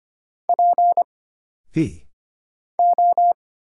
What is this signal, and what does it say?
Morse code, heard as a keyed tone, about 25 wpm, 700 Hz. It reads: P O
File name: Single Character (Letter:Number:Punctuation:Prosign) - Mind-Melt - 25wpm